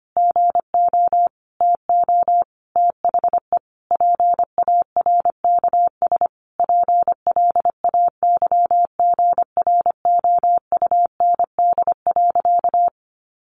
Morse code, heard as a keyed tone, about 25 wpm, 700 Hz. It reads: GO TO THE PARKS PLAYGROUND.
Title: Additional Easy Phrases 25wpm